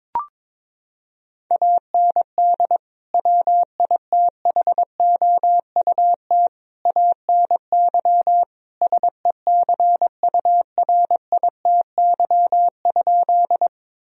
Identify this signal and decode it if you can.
Morse code, heard as a keyed tone, about 22 wpm, 700 Hz. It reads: AND WITHOUT ANY SECURITY?
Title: Sentences from Top 1000 Words - Review Part 4 - 22wpm